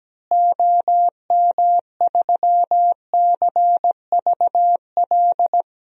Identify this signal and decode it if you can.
Morse code, heard as a keyed tone, about 17 wpm, 700 Hz. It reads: OM3CVL
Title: DXCC-100 Call Signs - 6 Characters - Repeated 3x - 17wpm